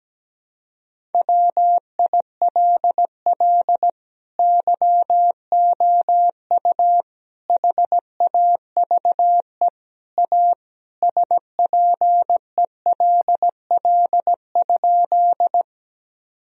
Morse code, heard as a keyed tone, about 17 wpm, 700 Hz. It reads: WILL YOU HAVE A SPELL?
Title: Sentences from Top 1000 Words - Review Part 1 - 17wpm